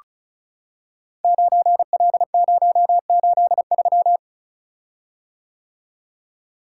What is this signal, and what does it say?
Morse code, heard as a keyed tone, about 36 wpm, 700 Hz. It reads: 9L083
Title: Code Groups - 5 Letter:Numbers - 35wpm